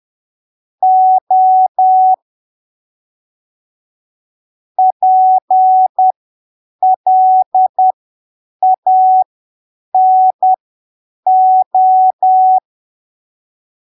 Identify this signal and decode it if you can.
Morse code, heard as a keyed tone, about 10 wpm, spaced as for 7 wpm, 750 Hz. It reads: O PLANO
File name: arquivos / texto / texto 07 10ppm